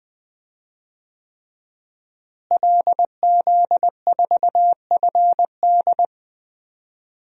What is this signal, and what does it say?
Morse code, heard as a keyed tone, about 20 wpm, 700 Hz. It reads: LZ4FD
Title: DXCC-100 Call Signs - 5 Characters - Repeated 3x - 20wpm